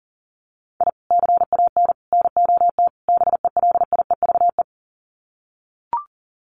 Morse code, heard as a keyed tone, about 40 wpm, 700 Hz. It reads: I CAN NOT BELIEVE
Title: Sets of 4 Words 40wpm